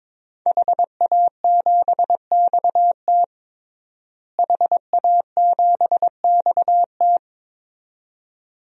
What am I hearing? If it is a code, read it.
Morse code, heard as a keyed tone, about 22 wpm, 700 Hz. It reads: HA7XT HA7XT
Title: DXCC-100 Call Signs - 5 Characters - Repeated 2x - 22wpm